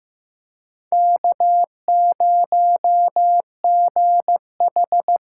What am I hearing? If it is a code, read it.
Morse code, heard as a keyed tone, about 15 wpm, 700 Hz. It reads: K0GH
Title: USA Call Signs - 5 Characters - 6x Character Spacing - 15wpm